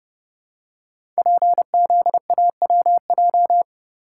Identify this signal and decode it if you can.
Morse code, heard as a keyed tone, about 30 wpm, 700 Hz. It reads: PZAWJ
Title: Code Groups - 5 Letter:Numbers - 30wpm